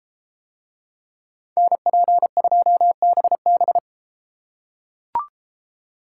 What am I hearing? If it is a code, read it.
Morse code, heard as a keyed tone, about 33 wpm, 700 Hz. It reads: NP2BB